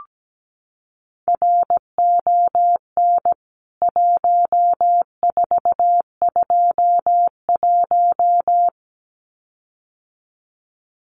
Morse code, heard as a keyed tone, about 17 wpm, 700 Hz. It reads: RON 1421
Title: CWT Exchange 17wpm